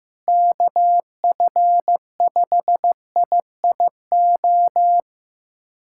Morse code, heard as a keyed tone, about 15 wpm, 700 Hz. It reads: KF5IIO
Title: USA Callsign DE Callsign - 6 Characters - 15wpm